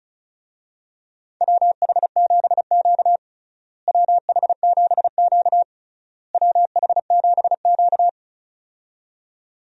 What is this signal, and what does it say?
Morse code, heard as a keyed tone, about 34 wpm, 700 Hz. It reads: WH7Q WH7Q WH7Q
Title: USA Call Signs - 4 Characters - Repeated 3x - 35wpm